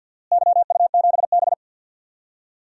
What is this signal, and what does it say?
Morse code, heard as a keyed tone, about 46 wpm, 700 Hz. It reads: YU7B
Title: DXCC-100 Call Signs - 4 Characters - Repeated 2x - 50wpm